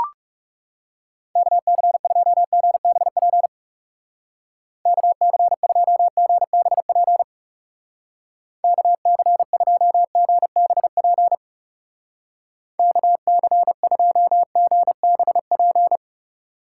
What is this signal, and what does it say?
Morse code, heard as a keyed tone, about 40 wpm, 700 Hz. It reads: KC2GBP KC2GBP KC2GBP KC2GBP